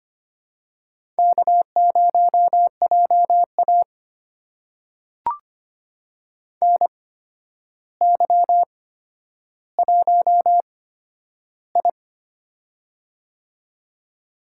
Morse code, heard as a keyed tone, about 25 wpm, spaced as for 11 wpm, 700 Hz. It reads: K0JA NY1I
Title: USA Call Signs - 4 Characters - 8x Character Spacing - 25wpm